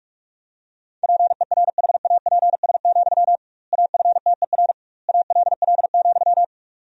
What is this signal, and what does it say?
Morse code, heard as a keyed tone, about 46 wpm, 700 Hz. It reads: PERHAPS, AFTER ALL,
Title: 2x Word-Spacing - Sentences from Top 500 Words - 45wpm